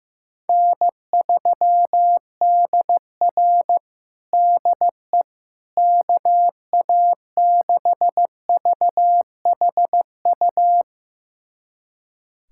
Morse code, heard as a keyed tone, about 15 wpm, 700 Hz. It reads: N3DR DE KA6VHU
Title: USA Callsign DE Callsign 15wpm